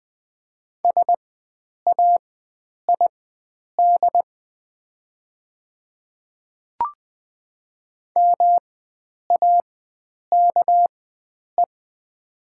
Morse code, heard as a keyed tone, about 20 wpm, spaced as for 9 wpm, 700 Hz. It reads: SAID MAKE
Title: 4x Character Spacing - Top 100 Words - 20wpm